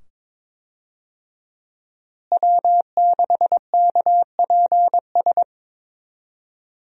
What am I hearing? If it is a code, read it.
Morse code, heard as a keyed tone, about 21 wpm, 700 Hz. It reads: W6KPS
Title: USA Call Signs - 5 Characters - Repeated 2x - 22wpm